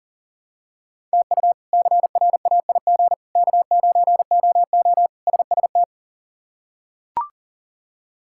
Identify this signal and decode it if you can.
Morse code, heard as a keyed tone, about 40 wpm, 700 Hz. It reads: TU CRAIG K9OO SST
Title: K1USN SST QSO Practice - 40wpm